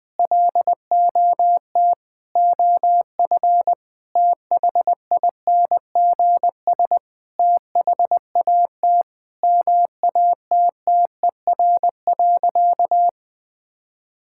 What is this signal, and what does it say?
Morse code, heard as a keyed tone, about 20 wpm, 700 Hz. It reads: LOT OF THINGS THAT MATTER.
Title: Sentences with 5-Word Phrases - Spoken Beginning - 20wpm